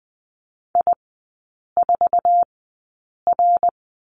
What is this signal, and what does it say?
Morse code, heard as a keyed tone, about 20 wpm, spaced as for 8 wpm, 700 Hz. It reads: I4R
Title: 138 Lesson 23e - Letter f - Three Characters 20wpm